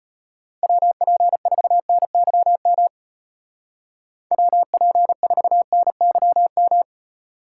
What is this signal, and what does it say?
Morse code, heard as a keyed tone, about 32 wpm, 700 Hz. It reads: WP4NYM WP4NYM